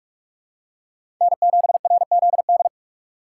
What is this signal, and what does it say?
Morse code, heard as a keyed tone, about 46 wpm, 700 Hz. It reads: N7RZD
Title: USA Call Signs - 5 Characters - 4x Character Spacing - 45wpm